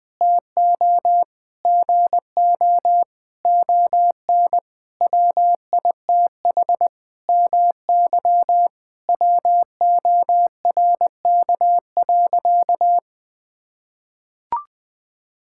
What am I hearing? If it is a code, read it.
Morse code, heard as a keyed tone, about 20 wpm, 700 Hz. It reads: TO GO ON WITH MY WORK.
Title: Sentences from Top 1000 Words - Review Part 6 - 20wpm